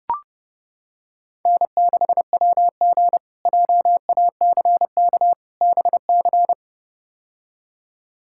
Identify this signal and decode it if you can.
Morse code, heard as a keyed tone, about 31 wpm, 700 Hz. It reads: N6WG JACK BC
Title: K1USN SST QSO Practice - 30wpm